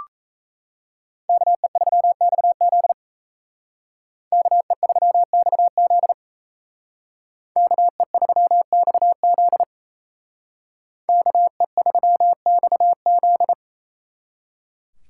Morse code, heard as a keyed tone, about 34 wpm, 700 Hz. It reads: KE3XZ KE3XZ KE3XZ KE3XZ
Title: Speed Racing - USA 2x2 Call Signs 28wpm